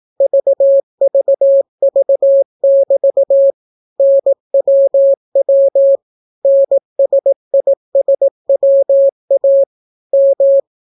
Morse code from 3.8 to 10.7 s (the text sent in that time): NWW NSISWA M